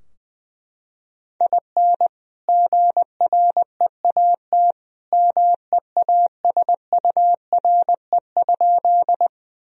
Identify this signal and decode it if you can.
Morse code, heard as a keyed tone, about 20 wpm, 700 Hz. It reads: IN GREAT MEASURE?
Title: Sentences from Top 1000 Words - Review Part 2 - 20wpm